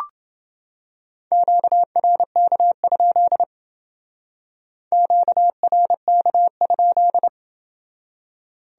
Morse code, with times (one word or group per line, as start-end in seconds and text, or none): 1.3-3.4 QRK?
4.9-7.3 QRK?